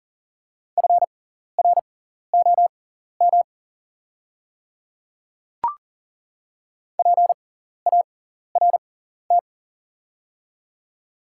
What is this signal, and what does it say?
Morse code, heard as a keyed tone, about 40 wpm, spaced as for 14 wpm, 700 Hz. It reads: FROM PART
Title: 6x Character Spacing - Top 200 Words - 40wpm